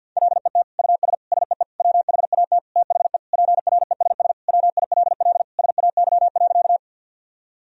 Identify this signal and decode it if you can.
Morse code, heard as a keyed tone, about 52 wpm, 700 Hz. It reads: LET US SEE WHAT THE PRESS WILL SAY.
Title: Sentences from Top 1000 Words - Review Part 1 - 50wpm